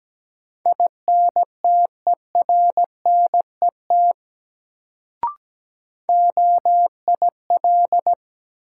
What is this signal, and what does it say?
Morse code, heard as a keyed tone, about 17 wpm, 700 Hz. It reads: INTERNET OIL
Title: Sentences from Top 1000 Words - Review Part 4 - New Word Drill - 17wpm